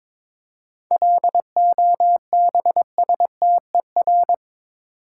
Morse code, heard as a keyed tone, about 27 wpm, 700 Hz. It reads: LOBSTER
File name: Easy Words - Speed-Racing with Hints - 22wpm